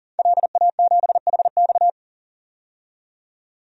Morse code, heard as a keyed tone, about 40 wpm, 700 Hz. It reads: LA7HX
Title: DXCC-100 Call Signs - 5 Characters - 40wpm